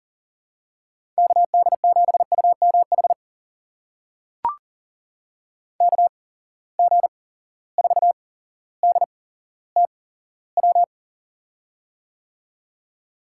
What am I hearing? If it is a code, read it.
Morse code, heard as a keyed tone, about 40 wpm, spaced as for 17 wpm, 700 Hz. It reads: KD7UMH KG4DTW
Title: USA Call Signs - 6 Characters - 8x Character Spacing - 40wpm